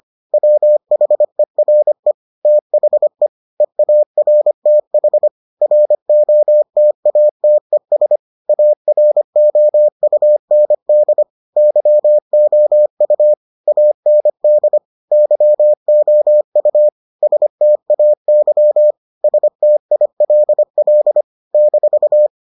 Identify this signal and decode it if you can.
Morse code, heard as a keyed tone, about 25 wpm, 600 Hz. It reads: WHERE THE EARTH ROTATES AROUND YOU AND YOU STAY STILL -